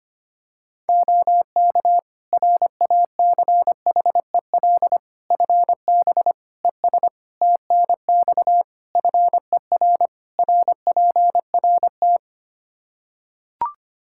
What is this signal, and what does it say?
Morse code, heard as a keyed tone, about 25 wpm, 700 Hz. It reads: OK RACHEL FB ES TNX FER RPRT
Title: K1USN SST QSO Practice 25wpm